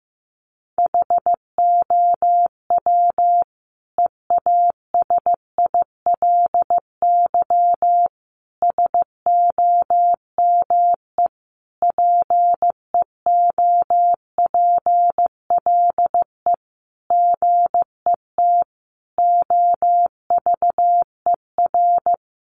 Morse code, heard as a keyed tone, about 15 wpm, 700 Hz. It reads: HOW EASILY SOME PEOPLE GET OVER
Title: Sentences from Top 900 Words 15wpm